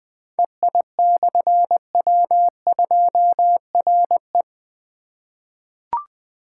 Morse code, heard as a keyed tone, about 20 wpm, 700 Hz. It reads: EI/W2RE
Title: Difficult Call Signs 20wpm